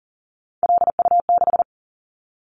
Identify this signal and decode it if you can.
Morse code, heard as a keyed tone, about 39 wpm, 700 Hz. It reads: LU6
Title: Callsign Prefixes for Top 100 DXCC - 40wpm